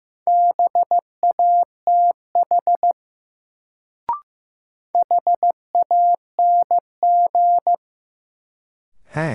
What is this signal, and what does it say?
Morse code, heard as a keyed tone, about 15 wpm, 700 Hz. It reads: BATH HANG
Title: Sentences from Top 1600 Words - New Word Drill - 15wpm